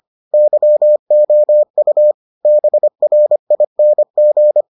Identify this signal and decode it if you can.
Morse code, heard as a keyed tone, about 25 wpm, 600 Hz. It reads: YOU BRING